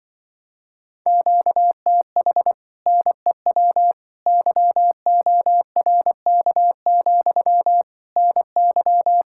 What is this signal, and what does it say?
Morse code, heard as a keyed tone, about 24 wpm, 700 Hz. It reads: QTH NEW YORK, NY
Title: Speed-Racing Common QSO Phrases - 20wpm